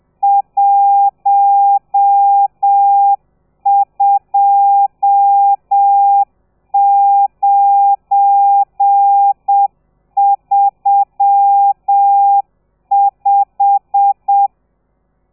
Morse code, heard as a keyed tone, about 7 wpm, 800 Hz. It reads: 12935
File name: telegrafia / i2tfj / Lez12 30